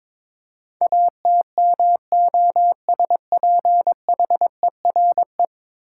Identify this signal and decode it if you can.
Morse code, heard as a keyed tone, about 22 wpm, 700 Hz. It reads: ATMOSPHERE
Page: morse-code-ninja-practice-22wpm